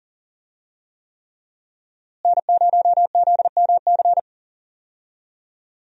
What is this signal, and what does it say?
Morse code, heard as a keyed tone, about 37 wpm, 700 Hz. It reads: N0ZMC
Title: USA Call Signs - 5 Characters - Repeated 3x - 40wpm